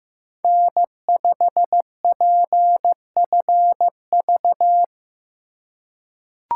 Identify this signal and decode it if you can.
Morse code, heard as a keyed tone, about 15 wpm, 700 Hz. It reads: N5PFV
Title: USA Call Signs - 4x Character Spacing - 15wpm